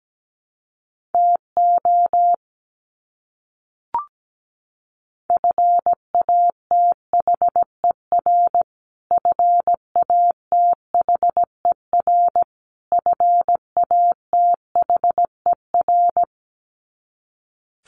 Morse code, heard as a keyed tone, about 17 wpm, 700 Hz. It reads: TO FATHER FATHER FATHER
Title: Top 300 Words Repeated 17wpm